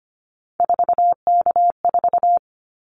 Morse code, heard as a keyed tone, about 25 wpm, 700 Hz. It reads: 4K4
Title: Callsign Prefixes for Top 100 DXCC - 25wpm